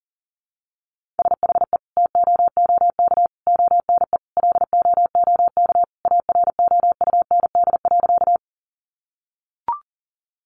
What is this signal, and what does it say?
Morse code, heard as a keyed tone, about 40 wpm, 700 Hz. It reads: SHE TOOK ONE LOOK AROUND.
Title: Sentences from Top 600 Words 40wpm